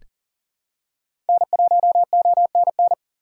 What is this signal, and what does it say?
Morse code, heard as a keyed tone, about 40 wpm, 700 Hz. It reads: N1ONN
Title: USA Call Signs - 5 Characters - 2x Character Spacing - 40wpm